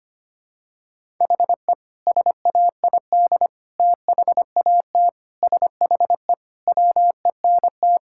Morse code, heard as a keyed tone, about 25 wpm, 700 Hz. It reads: HE SAID THAT SHE WENT
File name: practice / 25wpm / Sentences with 2-Word Phrases - 25wpm